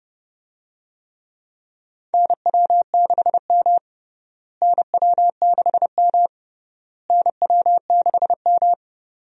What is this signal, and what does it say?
Morse code, heard as a keyed tone, about 29 wpm, 700 Hz. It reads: NW6M NW6M NW6M
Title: USA Call Signs - 4 Characters - Repeated 3x - 30wpm